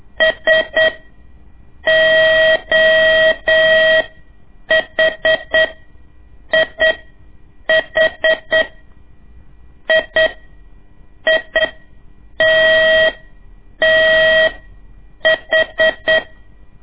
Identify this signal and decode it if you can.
Morse code, heard as a keyed tone, about 8 wpm, spaced as for 6 wpm, 640 Hz. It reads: SOHIH IITTH
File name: ik1weg2 / CORSI CW / lez2-1